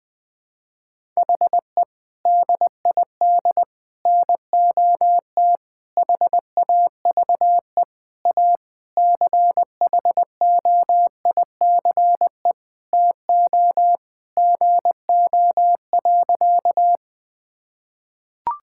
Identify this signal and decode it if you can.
Morse code, heard as a keyed tone, about 20 wpm, 700 Hz. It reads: HE DID NOT HAVE A CHOICE TO GO.
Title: Sentences with 4-Word Phrases - 20wpm